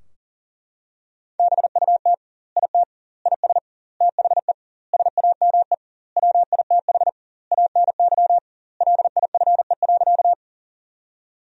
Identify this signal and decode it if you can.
Morse code, heard as a keyed tone, about 40 wpm, 700 Hz. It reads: BUT IT IS THE SAME WITH ANY LIFE.
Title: 2x Word-Spacing - Sentences from Top 300 Words - 40wpm